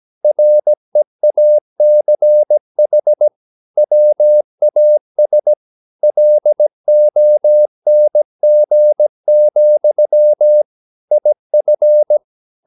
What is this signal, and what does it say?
Morse code, heard as a keyed tone, about 17 wpm, 600 Hz. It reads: REACH WAS LONG, IF
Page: Romanen Never af Ken Follet på engelsk.